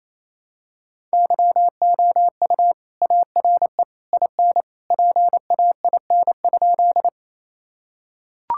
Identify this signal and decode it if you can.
Morse code, heard as a keyed tone, about 28 wpm, 700 Hz. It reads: YOU ARE IN PAIN?
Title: Sentences from Top 1600 Words 28wpm